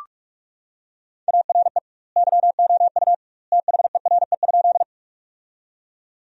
Morse code, heard as a keyed tone, about 46 wpm, 700 Hz. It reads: ARE YOU THERE?
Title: 2x Word-Spacing - Sentences from Top 100 Words - 45wpm